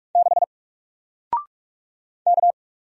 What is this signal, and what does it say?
Morse code, heard as a keyed tone, about 44 wpm, 700 Hz. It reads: 6 K
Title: Single Letter-Number Alt 45wpm